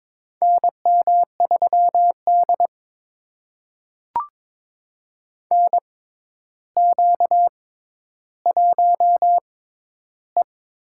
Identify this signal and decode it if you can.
Morse code, heard as a keyed tone, about 21 wpm, spaced as for 11 wpm, 700 Hz. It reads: NM3D NQ1E
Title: USA Call Signs - 4 Characters - 6x Character Spacing - 22wpm